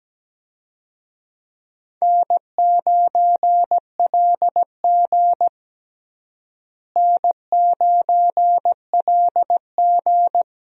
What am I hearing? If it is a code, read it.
Morse code, heard as a keyed tone, about 17 wpm, 700 Hz. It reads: N9LG N9LG
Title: USA Call Signs - 4 Characters - Repeated 2x - 17wpm